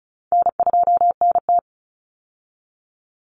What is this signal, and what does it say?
Morse code, heard as a keyed tone, about 34 wpm, 700 Hz. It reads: N2NT
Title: Easy Call Signs Repeated 3x - 35wpm